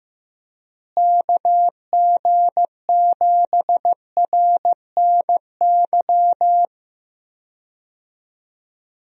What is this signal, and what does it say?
Morse code, heard as a keyed tone, about 15 wpm, 700 Hz. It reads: KG7RNY
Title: USA Call Signs - 6 Characters - Repeated 3x - 15wpm